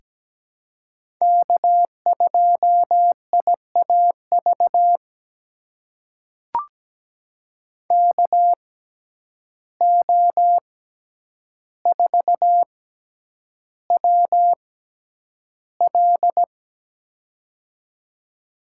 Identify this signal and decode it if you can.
Morse code, heard as a keyed tone, about 17 wpm, spaced as for 9 wpm, 700 Hz. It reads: K2IAV KO4WL
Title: USA Call Signs - 5 Characters - 6x Character Spacing - 17wpm